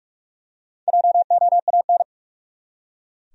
Morse code, heard as a keyed tone, about 43 wpm, 700 Hz. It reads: JOAN
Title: USA Common Names - Max 4 Letters - 45wpm